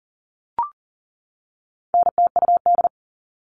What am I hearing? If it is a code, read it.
Morse code, heard as a keyed tone, about 38 wpm, 700 Hz. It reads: NTUD
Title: All Cut Numbers - 4-Digit 40wpm